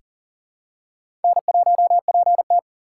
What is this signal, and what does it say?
Morse code, heard as a keyed tone, about 38 wpm, 690 Hz. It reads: N1PT
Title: USA Call Signs - 4x Character Spacing - 40wpm